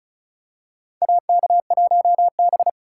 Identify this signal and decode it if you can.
Morse code, heard as a keyed tone, about 36 wpm, 700 Hz. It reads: AK1B
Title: USA Call Signs - 4 Characters - 4x Character Spacing - 35wpm